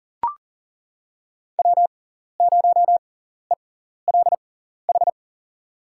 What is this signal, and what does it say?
Morse code, heard as a keyed tone, about 40 wpm, spaced as for 14 wpm, 700 Hz. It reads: W0ELH
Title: USA Call Signs - 5 Characters - 6x Character Spacing - 40wpm